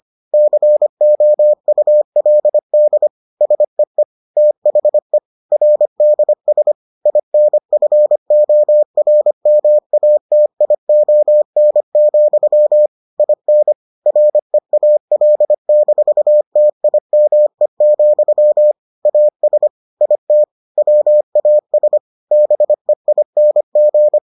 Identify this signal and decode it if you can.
Morse code, heard as a keyed tone, about 25 wpm, 600 Hz. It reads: COULD SEE THE RDS INFORMATION, IN REAL-TIME, AS IT WAS BEING